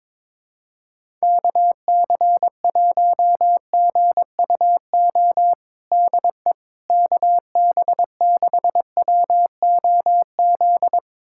Morse code, heard as a keyed tone, about 21 wpm, 700 Hz. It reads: KC1GUO DE KB6WOZ
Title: USA Callsign DE Callsign 22wpm